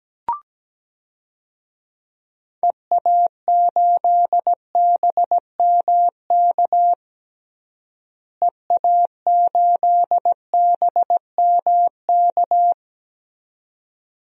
Morse code, heard as a keyed tone, about 17 wpm, 700 Hz. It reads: EA8BMK EA8BMK
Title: DXCC-100 Call Signs - 6 Characters - Repeated 2x - 17wpm